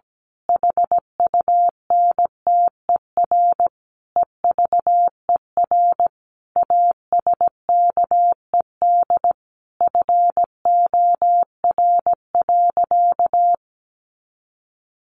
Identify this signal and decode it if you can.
Morse code, heard as a keyed tone, about 17 wpm, 700 Hz. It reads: HUNTER EVER ASKED FOR.
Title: A Sound Of Thunder 17wpm